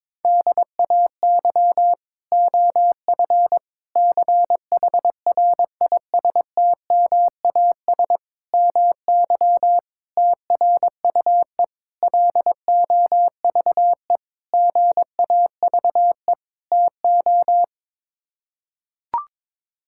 Morse code, heard as a keyed tone, about 22 wpm, 700 Hz. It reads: DAY OF CHRISTMAS MY TRUE LOVE GAVE TO
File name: practice / 22wpm / Sets of 8 Words - 22wpm